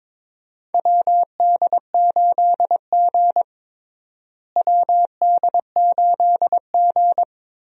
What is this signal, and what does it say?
Morse code, heard as a keyed tone, about 22 wpm, 700 Hz. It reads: WD8G WD8G
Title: USA Call Signs - 4 Characters - Repeated 2x - 22wpm